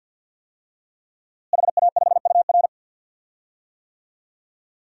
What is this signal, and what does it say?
Morse code, heard as a keyed tone, about 48 wpm, 700 Hz. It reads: HA5UR
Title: DXCC-100 Call Signs - 5 Characters - Repeated 2x - 50wpm